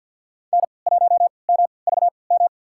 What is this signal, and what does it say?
Morse code, heard as a keyed tone, about 49 wpm, spaced as for 28 wpm, 700 Hz. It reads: N1MVM
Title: USA Call Signs - 5 Characters - 3x Character Spacing - 50wpm